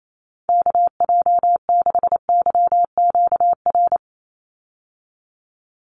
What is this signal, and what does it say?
Morse code, heard as a keyed tone, about 28 wpm, 700 Hz. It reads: KJ6YQR
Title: USA Call Signs 28wpm